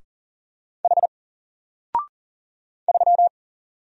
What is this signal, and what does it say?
Morse code, heard as a keyed tone, about 41 wpm, 700 Hz. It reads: H 3
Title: Single Character (Letter:Number:Punctuation:Prosign) 40wpm